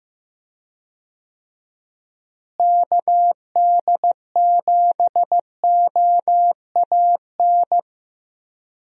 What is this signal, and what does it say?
Morse code, heard as a keyed tone, about 15 wpm, 700 Hz. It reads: KD7OAN
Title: USA Call Signs - 6 Characters - Repeated 3x - 15wpm